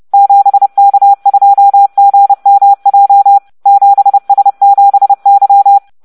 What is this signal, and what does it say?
Morse code, heard as a keyed tone, about 30 wpm, 800 Hz. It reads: 7K2GMJ 7S7Y